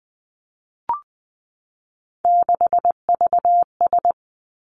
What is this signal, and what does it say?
Morse code, heard as a keyed tone, about 20 wpm, 700 Hz. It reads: 6VS
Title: Three Challenging Letter-Number Combinations 20wpm